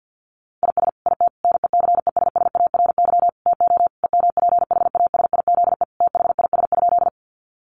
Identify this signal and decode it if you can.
Morse code, heard as a keyed tone, about 50 wpm, 700 Hz. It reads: IS IT NECESSARY TO EMPHASIZE THIS?
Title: Sentences from Top 1500 Words 50wpm